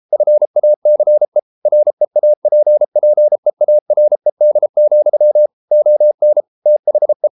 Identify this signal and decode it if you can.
Morse code, heard as a keyed tone, about 33 wpm, 600 Hz. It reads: FACE REAPPEARED, ON THE